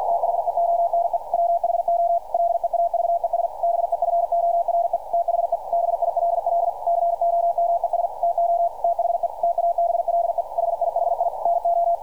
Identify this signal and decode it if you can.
Morse code, heard as a keyed tone, about 48 wpm, 690 Hz, with radio noise. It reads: THE ONE WHO LEAVES BEFORE THE BILL COMES TO THE TABLE HIHI NO
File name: archived / CRSnet